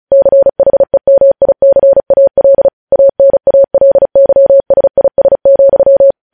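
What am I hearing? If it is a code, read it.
Morse code, heard as a keyed tone, about 36 wpm, 550 Hz. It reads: CHEMICAL ANALYSIS,
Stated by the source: Morse news headlines practice at 35.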